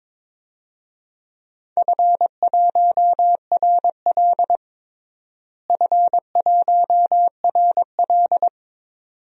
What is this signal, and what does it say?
Morse code, heard as a keyed tone, about 22 wpm, 700 Hz. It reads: F1RL F1RL
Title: DXCC-100 Call Signs - 4 Characters - Repeated 2x - 22wpm